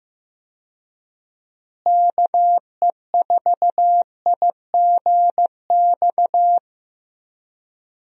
Morse code, heard as a keyed tone, about 15 wpm, 700 Hz. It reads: KE4IGX
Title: USA Call Signs - 6 Characters - Repeated 3x - 15wpm